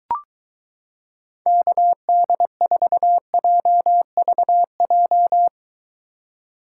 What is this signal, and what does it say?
Morse code, heard as a keyed tone, about 22 wpm, 700 Hz. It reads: KD4JVJ